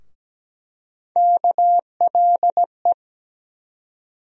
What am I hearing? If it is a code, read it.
Morse code, heard as a keyed tone, about 17 wpm, 700 Hz. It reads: KLE
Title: USA Call Sign Postfixes - 3 Letters 17wpm